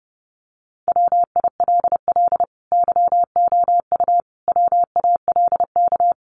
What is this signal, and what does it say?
Morse code, heard as a keyed tone, about 30 wpm, 700 Hz. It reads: WILL YOU WALK
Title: Sentences from Top 1100 Words 30wpm